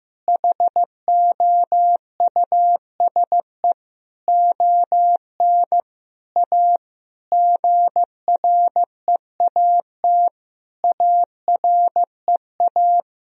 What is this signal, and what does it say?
Morse code, heard as a keyed tone, about 15 wpm, 700 Hz. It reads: HOUSE ON A GREAT AREA
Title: Sentences from Top 1000 Words - Review Part 2 - 15wpm